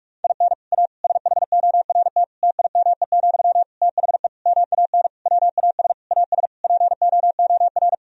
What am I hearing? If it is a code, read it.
Morse code, heard as a keyed tone, about 42 wpm, 700 Hz. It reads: IN A SHORT TIME, THE MAN WAS AS POOR
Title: Sentences from Top 1000 Words - Review Part 2 - 45wpm